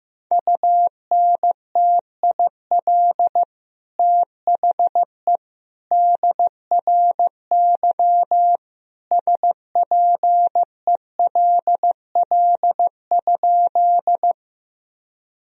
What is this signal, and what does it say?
Morse code, heard as a keyed tone, about 15 wpm, 700 Hz. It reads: UNTIL THE DRY SPELL?
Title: Sentences from Top 1000 Words - Review Part 3 - 15wpm